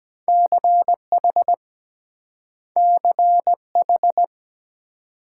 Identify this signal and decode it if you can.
Morse code, heard as a keyed tone, about 24 wpm, 700 Hz. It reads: CH CH